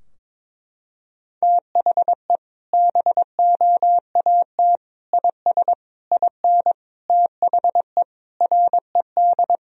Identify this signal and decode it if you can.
Morse code, heard as a keyed tone, about 22 wpm, 700 Hz. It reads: THE BOAT IS IN THE RED
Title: Sentences from Top 1000 Words - Review Part 2 - 22wpm